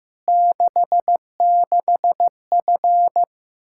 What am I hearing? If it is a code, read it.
Morse code, heard as a keyed tone, about 15 wpm, 700 Hz. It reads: 66F
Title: Practice QSO - Exchange 2 - Info - 15wpm